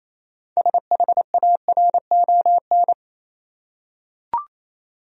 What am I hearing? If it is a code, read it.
Morse code, heard as a keyed tone, about 28 wpm, 700 Hz. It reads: SHARON
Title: USA Common Names - 28wpm